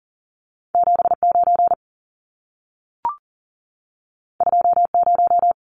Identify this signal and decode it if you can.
Morse code, heard as a keyed tone, about 40 wpm, 700 Hz. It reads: 79 20
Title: Numbers - 2-Digit 40wpm